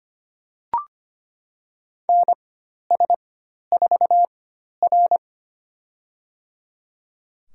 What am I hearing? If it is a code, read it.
Morse code, heard as a keyed tone, about 25 wpm, spaced as for 12 wpm, 700 Hz. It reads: NS4R
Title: USA Call Signs - 4x Character Spacing - 25wpm